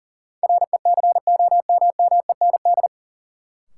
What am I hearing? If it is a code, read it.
Morse code, heard as a keyed tone, about 40 wpm, 700 Hz. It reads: RECOMMEND
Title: Sentences from Top 1200 Words - New Word Drill - 40wpm